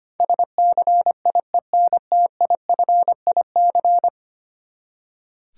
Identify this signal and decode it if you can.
Morse code, heard as a keyed tone, about 25 wpm, 700 Hz. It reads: SCIENTIFIC
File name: Ten Letter Words - 25wpm